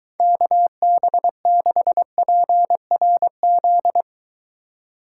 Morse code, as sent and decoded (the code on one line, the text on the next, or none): -.- -... -.... .--. .-. --..
KB6PRZ